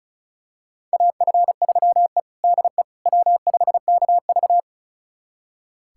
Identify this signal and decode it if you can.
Morse code, heard as a keyed tone, about 36 wpm, 700 Hz. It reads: AF3E DE W5KV
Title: USA Callsign DE Callsign - 4 Characters - 35wpm